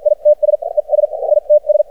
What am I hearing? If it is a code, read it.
Morse code, heard as a keyed tone, about 53 wpm, spaced as for 42 wpm, 600 Hz. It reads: ITS ES ITS
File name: archived / CRSnet